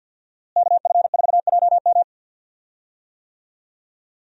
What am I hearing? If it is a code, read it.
Morse code, heard as a keyed tone, about 53 wpm, 700 Hz. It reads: KF4JM
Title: USA Call Signs - 5 Characters - Repeated 3x - 50wpm